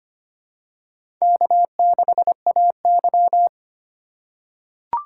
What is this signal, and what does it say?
Morse code, heard as a keyed tone, about 25 wpm, 700 Hz. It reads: K6AY
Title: USA Call Signs - 6x Character Spacing - 25wpm